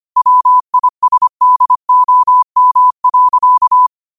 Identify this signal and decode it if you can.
Morse code, heard as a keyed tone, about 25 wpm, 1000 Hz. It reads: WISDOM.
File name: qotdfiles / 25wpm